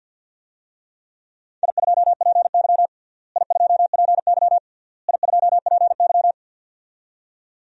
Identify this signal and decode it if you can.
Morse code, heard as a keyed tone, about 46 wpm, 700 Hz. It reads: I2PY I2PY I2PY
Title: DXCC-100 Call Signs - 4 Characters - Repeated 3x - 50wpm